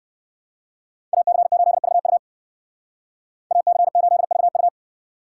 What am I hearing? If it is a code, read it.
Morse code, heard as a keyed tone, about 56 wpm, 700 Hz. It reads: AC8VU AC8VU
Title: Speed Racing - USA 2x2 Call Signs 45wpm